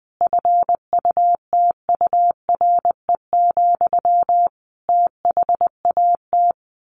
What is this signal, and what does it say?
Morse code, heard as a keyed tone, about 20 wpm, 700 Hz. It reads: FUTURE, THAT
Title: A Sound Of Thunder 20wpm